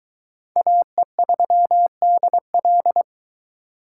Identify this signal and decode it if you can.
Morse code, heard as a keyed tone, about 22 wpm, 700 Hz. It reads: AE3DL